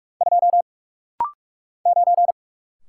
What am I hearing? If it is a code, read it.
Morse code, heard as a keyed tone, about 46 wpm, 700 Hz. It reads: 2 9
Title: Single Letter-Number - Rapid-Fire 45wpm